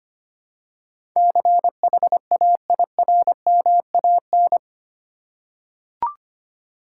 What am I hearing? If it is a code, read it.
Morse code, heard as a keyed tone, about 25 wpm, 700 Hz. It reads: CHAIRMAN
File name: Eight Letter Words - 25wpm